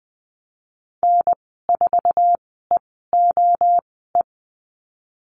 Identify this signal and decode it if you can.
Morse code, heard as a keyed tone, about 19 wpm, spaced as for 14 wpm, 700 Hz. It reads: N4EOE
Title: 076 Lesson 13g - Letter L - Call Signs 20wpm